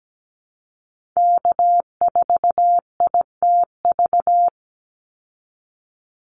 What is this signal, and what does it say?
Morse code, heard as a keyed tone, about 17 wpm, 700 Hz. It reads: K4ITV
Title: Call Signs Repeated 3x - 17wpm